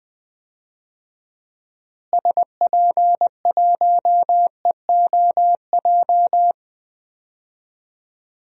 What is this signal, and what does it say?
Morse code, heard as a keyed tone, about 20 wpm, 700 Hz. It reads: SP1EOJ
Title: DXCC-100 Call Signs - 6 Characters - Repeated 3x - 20wpm